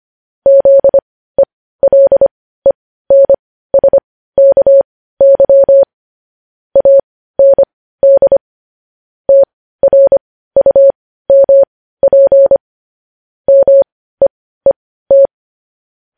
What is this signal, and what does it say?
Morse code, heard as a keyed tone, about 25 wpm, spaced as for 15 wpm, 550 Hz. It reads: ZELENSKY AND TRUMP MEET
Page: News headlines at 15